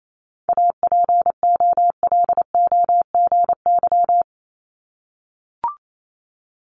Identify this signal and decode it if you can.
Morse code, heard as a keyed tone, about 28 wpm, 700 Hz. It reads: APOLOGY
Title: Multi-letter Phonemes 28wpm